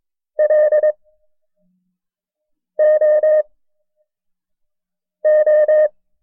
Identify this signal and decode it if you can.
Morse code, heard as a keyed tone, about 22 wpm, 610 Hz. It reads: L O O
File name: FreeFiles / CW-practice / KB6MT-tape1-side2